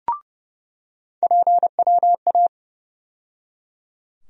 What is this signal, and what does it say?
Morse code, heard as a keyed tone, about 31 wpm, 700 Hz. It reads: PWA
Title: USA Call Sign Postfixes - 3 Letters 30wpm